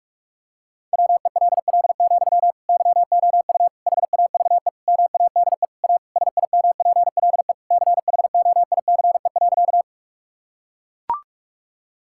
Morse code, heard as a keyed tone, about 41 wpm, 700 Hz. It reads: WELL, YOU HAVE MADE A SIMPLE CHOICE.
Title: Sentences from Top 1000 Words - Review Part 6 - 45wpm